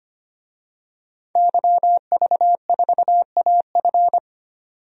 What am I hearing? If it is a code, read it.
Morse code, heard as a keyed tone, about 25 wpm, 700 Hz. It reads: YV4AF
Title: DXCC-100 Call Signs - 5 Characters - 25wpm